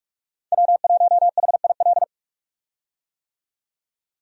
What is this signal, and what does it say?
Morse code, heard as a keyed tone, about 46 wpm, 700 Hz. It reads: W1HIL
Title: USA Call Signs - 5 Characters - 45wpm